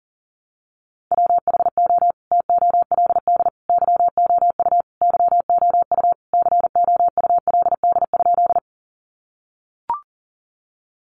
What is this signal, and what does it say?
Morse code, heard as a keyed tone, about 40 wpm, 700 Hz. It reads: WHO TOLD YOU YOU COULD?
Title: Sentences from Top 400 Words 40wpm